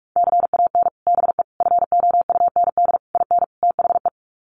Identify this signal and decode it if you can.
Morse code, heard as a keyed tone, about 44 wpm, 700 Hz. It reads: CAN BE FOUND IN THE
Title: Sets of 5 Words Encore 45wpm